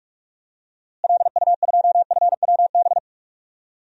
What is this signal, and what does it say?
Morse code, heard as a keyed tone, about 44 wpm, 700 Hz. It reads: LU2FWB
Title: DXCC-100 Call Signs - 6 Characters - Repeated 2x - 45wpm